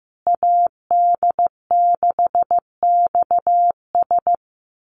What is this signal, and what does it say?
Morse code, heard as a keyed tone, about 15 wpm, 700 Hz. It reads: AD6XS
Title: USA Call Signs Repeated 2x 15wpm